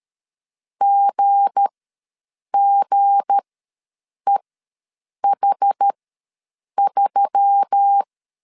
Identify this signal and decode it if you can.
Morse code, heard as a keyed tone, about 13 wpm, spaced as for 7 wpm, 800 Hz. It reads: GGEH3
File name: LEZIONI / 7 GF9Z / 7 G ws 12wpm 3140